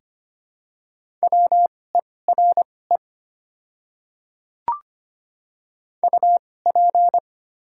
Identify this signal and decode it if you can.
Morse code, heard as a keyed tone, about 25 wpm, spaced as for 18 wpm, 700 Hz. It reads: WERE UP
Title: 2x Character Spacing - Top 100 Words - 25wpm